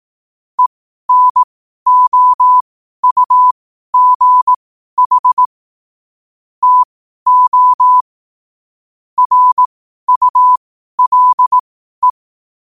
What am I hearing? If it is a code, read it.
Morse code, heard as a keyed tone, about 18 wpm, spaced as for 12 wpm, 1000 Hz. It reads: ENOUGH TO RULE
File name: qotdfiles / 12wpm